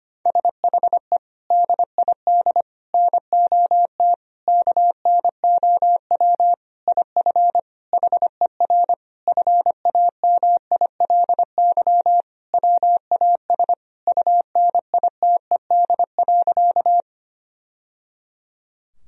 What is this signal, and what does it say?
Morse code, heard as a keyed tone, about 25 wpm, 700 Hz. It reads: SHE DID NOT KNOW IF HER FAMILY WAS UNITED.
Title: Sentences from Top 1000 Words - Review Part 4 - 25wpm